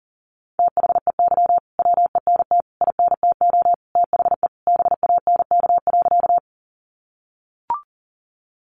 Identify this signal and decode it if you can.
Morse code, heard as a keyed tone, about 40 wpm, 700 Hz. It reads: THEY WENT INTO THE BANK.
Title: Sentences from Top 900 Words 40wpm